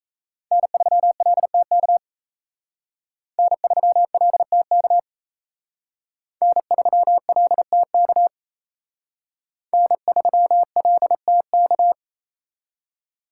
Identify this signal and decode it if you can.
Morse code, heard as a keyed tone, about 35 wpm, 700 Hz. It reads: N3LTK N3LTK N3LTK N3LTK
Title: Speed Racing - USA 1x3 Call Signs 28wpm